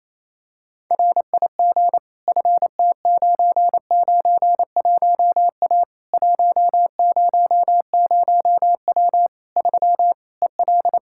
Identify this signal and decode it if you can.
Morse code, heard as a keyed tone, about 28 wpm, 700 Hz. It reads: RIG FT991A 100W 3 EL
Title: K1USN SST QSO Practice 28wpm